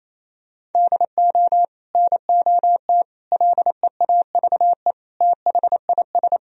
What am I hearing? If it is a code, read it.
Morse code, heard as a keyed tone, about 28 wpm, 700 Hz. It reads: DO NOT LEAVE THIS
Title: Sentences from Top 1000 Words - Review Part 1 - 28wpm